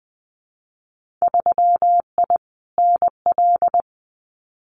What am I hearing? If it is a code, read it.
Morse code, heard as a keyed tone, about 20 wpm, 700 Hz. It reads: 3I NL
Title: Winter Field Day Exchanges 20wpm